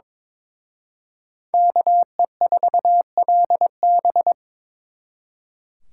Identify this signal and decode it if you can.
Morse code, heard as a keyed tone, about 26 wpm, 700 Hz. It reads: KE4LB
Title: Speed Racing - USA Call Signs 22wpm